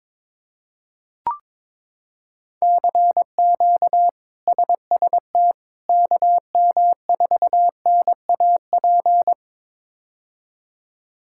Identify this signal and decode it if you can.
Morse code, heard as a keyed tone, about 22 wpm, 700 Hz. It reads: CQ SST KM4NAP
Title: K1USN SST QSO Practice - 22wpm